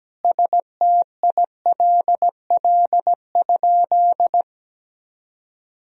Morse code, heard as a keyed tone, about 17 wpm, 700 Hz. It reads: STILL?
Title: Sentences from Top 1000 Words - Review Part 3 - 17wpm